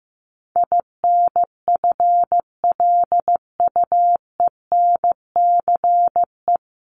Morse code, heard as a keyed tone, about 15 wpm, 700 Hz. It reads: INFLUENCE
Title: Top 1001-1250 Words 15wpm